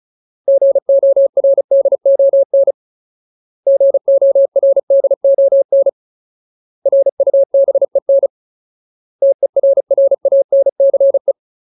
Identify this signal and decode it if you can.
Morse code, heard as a keyed tone, about 34 wpm, 550 Hz. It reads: GORDON GORDON RUBEN TERRANCE
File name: Ultimate Word List / 35WPM Ultimate